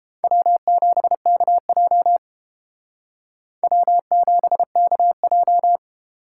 Morse code, heard as a keyed tone, about 31 wpm, 700 Hz. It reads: W7KJ W7KJ